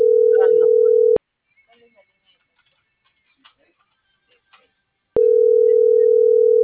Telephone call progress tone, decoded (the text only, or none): ringback tone